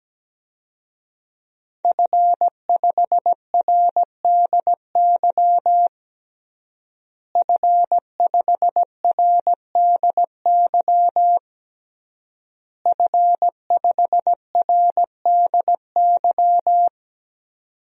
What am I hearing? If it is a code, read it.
Morse code, heard as a keyed tone, about 17 wpm, 700 Hz. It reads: F5RDY F5RDY F5RDY
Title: DXCC-100 Call Signs - 5 Characters - Repeated 3x - 17wpm